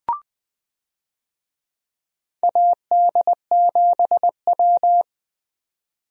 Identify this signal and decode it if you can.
Morse code, heard as a keyed tone, about 20 wpm, 700 Hz. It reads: AD7W
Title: USA Call Signs - 4 Characters - 20wpm